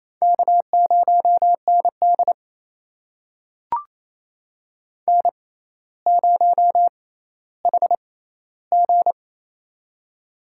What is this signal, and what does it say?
Morse code, heard as a keyed tone, about 28 wpm, spaced as for 14 wpm, 700 Hz. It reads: K0ND N0HG
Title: USA Call Signs - 4 Characters - 6x Character Spacing - 28wpm